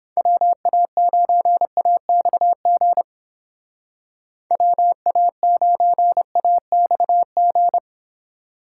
Morse code, heard as a keyed tone, about 29 wpm, 700 Hz. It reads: WA9AXG WA9AXG